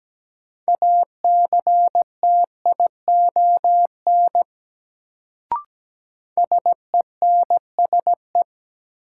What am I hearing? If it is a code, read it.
Morse code, heard as a keyed tone, about 17 wpm, 700 Hz. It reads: ACTION SENSE
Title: Sentences from Top 700 Words - New Word Drill - 17wpm